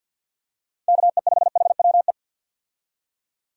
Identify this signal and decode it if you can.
Morse code, heard as a keyed tone, about 49 wpm, 700 Hz. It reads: KE5HWE
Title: USA Call Signs - 6 Characters - 50wpm